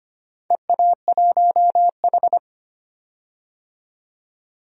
Morse code, heard as a keyed tone, about 25 wpm, 700 Hz. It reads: EA1H
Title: DXCC-100 Call Signs - 4 Characters - 25wpm